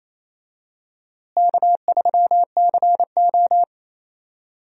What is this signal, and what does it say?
Morse code, heard as a keyed tone, about 27 wpm, 700 Hz. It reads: K3CO
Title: USA Call Signs - 4 Characters - Repeated 2x - 28wpm